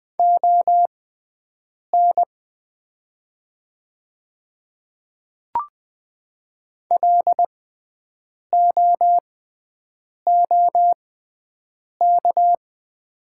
Morse code, heard as a keyed tone, about 20 wpm, spaced as for 7 wpm, 700 Hz. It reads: ON LOOK
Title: 6x Character Spacing - Top 100 Words - 20wpm